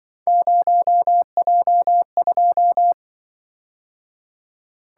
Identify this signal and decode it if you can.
Morse code, heard as a keyed tone, about 24 wpm, 700 Hz. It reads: 0J2
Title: Speed-Racing - Three Letter-Number Combinations - 20wpm